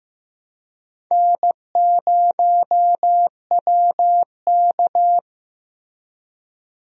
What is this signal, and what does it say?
Morse code, heard as a keyed tone, about 15 wpm, 700 Hz. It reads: N0WK
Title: USA Call Signs - 4 Characters - Repeated 2x - 15wpm